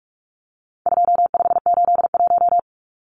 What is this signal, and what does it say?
Morse code, heard as a keyed tone, about 44 wpm, 700 Hz. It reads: 2581
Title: Numbers - 4-Digit 45wpm